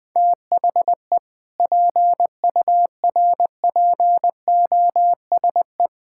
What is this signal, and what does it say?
Morse code, heard as a keyed tone, about 20 wpm, 700 Hz. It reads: THE PURPOSE
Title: Sentences with 5-Word Phrases - 20wpm